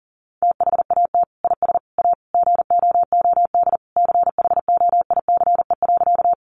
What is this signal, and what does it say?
Morse code, heard as a keyed tone, about 40 wpm, 700 Hz. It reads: THAT IS A GOOD CHOICE.
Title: Sentences from Top 900 Words 40wpm